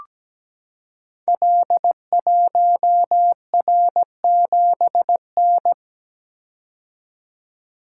Morse code, heard as a keyed tone, about 17 wpm, 690 Hz. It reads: L1R7N
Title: Code Groups - 5 Letter:Numbers - 17wpm